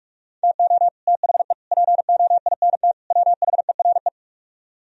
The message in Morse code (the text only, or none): TO THE POINT WHERE